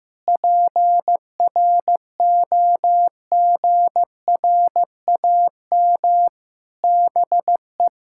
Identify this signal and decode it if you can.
Morse code, heard as a keyed tone, about 15 wpm, 700 Hz. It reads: PROGRAM BE
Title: Sentences from Top 1000 Words - Review Part 5 - 15wpm